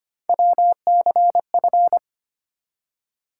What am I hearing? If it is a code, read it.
Morse code, heard as a keyed tone, about 25 wpm, 700 Hz. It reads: WCF
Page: morse-code-ninja-practice-25wpm